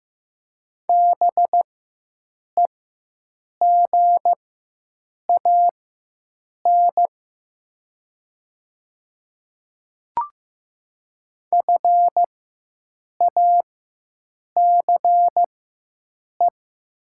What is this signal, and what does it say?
Morse code, heard as a keyed tone, about 15 wpm, spaced as for 7 wpm, 700 Hz. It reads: BEGAN FACE
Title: 4x Character Spacing - Top 300 Words - 15wpm